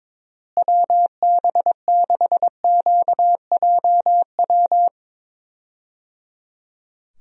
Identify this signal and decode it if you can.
Morse code, heard as a keyed tone, about 22 wpm, 700 Hz. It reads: WB6QJW
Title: USA Call Signs - 6 Characters - Repeated 3x - 22wpm